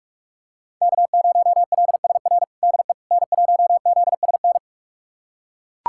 Morse code, heard as a keyed tone, about 42 wpm, 700 Hz. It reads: K0LSR DE N1ZSN
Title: USA Callsign DE Callsign - 5 Characters - 45wpm